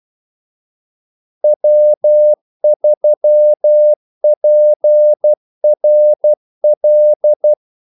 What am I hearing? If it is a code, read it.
Morse code, heard as a keyed tone, about 12 wpm, 600 Hz. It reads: W3PRL
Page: Kaldesignaler 12 wpm